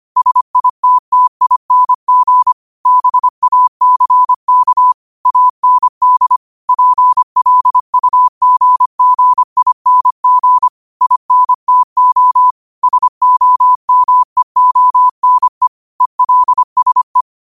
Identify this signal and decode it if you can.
Morse code, heard as a keyed tone, about 25 wpm, 1000 Hz. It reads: SITTING BACK AND PLUGGING INTO SOMEONE ELSE